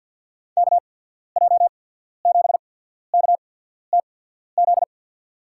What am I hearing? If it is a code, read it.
Morse code, heard as a keyed tone, about 50 wpm, spaced as for 14 wpm, 700 Hz. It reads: KJ7KTZ
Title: USA Call Signs - 6 Characters - 8x Character Spacing - 50wpm